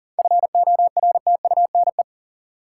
Morse code, heard as a keyed tone, about 39 wpm, 700 Hz. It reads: FORTUNE
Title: Seven Letter Words - 40wpm